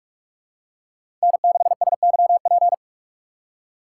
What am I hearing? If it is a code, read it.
Morse code, heard as a keyed tone, about 46 wpm, 700 Hz. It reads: N6SYP
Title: USA Call Signs - 5 Characters - 2x Character Spacing - 45wpm